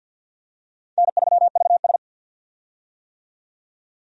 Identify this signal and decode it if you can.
Morse code, heard as a keyed tone, about 50 wpm, 700 Hz. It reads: N3VS
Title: USA Call Signs - 4 Characters - 50wpm